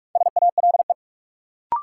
Morse code, heard as a keyed tone, about 46 wpm, 700 Hz. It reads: SALE
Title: Sentences from Top 1400 Words - New Word Drill - 45wpm